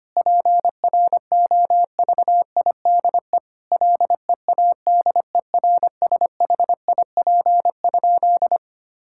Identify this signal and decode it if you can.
Morse code, heard as a keyed tone, about 25 wpm, 700 Hz. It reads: PROVIDE LEADERSHIP?
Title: Sentences from Top 1600 Words 25wpm